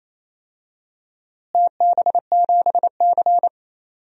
Morse code, heard as a keyed tone, about 28 wpm, 700 Hz. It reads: TB7C
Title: DXCC-100 Call Signs - 4 Characters - Repeated 3x - 28wpm